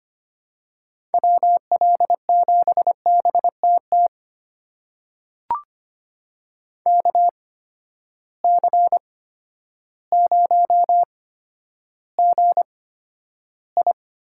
Morse code, heard as a keyed tone, about 25 wpm, spaced as for 11 wpm, 700 Hz. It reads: WL7BTT KC0GI
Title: USA Call Signs - 6 Characters - 8x Character Spacing - 25wpm